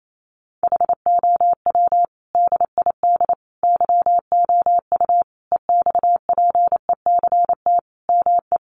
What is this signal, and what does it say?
Morse code, heard as a keyed tone, about 28 wpm, 700 Hz. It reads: HOW DID YOU EXPECT ME
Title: Sentences from Top 1100 Words 28wpm